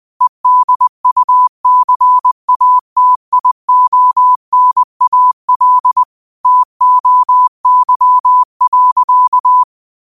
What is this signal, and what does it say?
Morse code, heard as a keyed tone, about 20 wpm, 1000 Hz. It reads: EDUCATIONAL TOY.